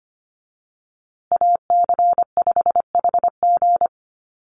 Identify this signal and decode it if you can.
Morse code, heard as a keyed tone, about 25 wpm, 700 Hz. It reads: AC5HG
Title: USA Call Signs Repeated 2x 25wpm